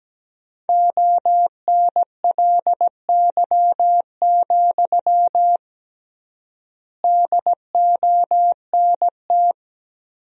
Morse code, heard as a keyed tone, about 17 wpm, 700 Hz. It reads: ONLY, DONT
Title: 3x Word-Spacing - Sentences from Top 300 Words - 17wpm